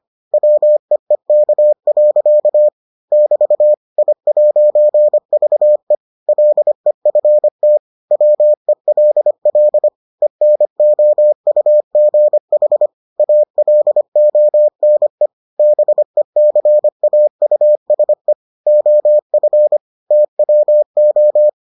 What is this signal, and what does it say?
Morse code, heard as a keyed tone, about 25 wpm, 600 Hz. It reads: WEEK. <BT> I'VE LEFT WELL ENOUGH ALONE BECAUSE OF TWO